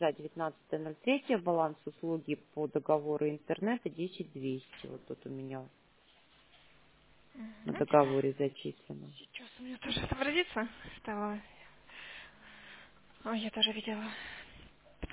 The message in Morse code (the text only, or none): none